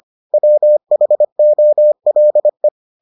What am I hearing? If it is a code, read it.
Morse code, heard as a keyed tone, about 25 wpm, 600 Hz. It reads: WHOLE